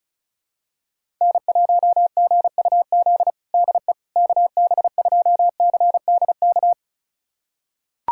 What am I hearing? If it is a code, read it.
Morse code, heard as a keyed tone, about 34 wpm, 700 Hz. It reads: N1GUZ DE KB2CDK
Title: USA Callsign DE Callsign 35wpm